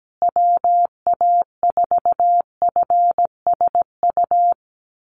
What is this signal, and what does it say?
Morse code, heard as a keyed tone, about 17 wpm, 700 Hz. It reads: WA4FSU
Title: USA Call Signs Repeated 2x 17wpm